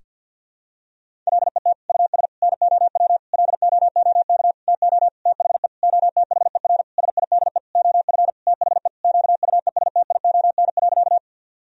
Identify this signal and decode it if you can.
Morse code, heard as a keyed tone, about 52 wpm, 700 Hz. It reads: LET US NOW LOOK TO THE OTHER SIDE OF THE QUESTION.
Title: Sentences from Top 1000 Words - Review Part 1 - 50wpm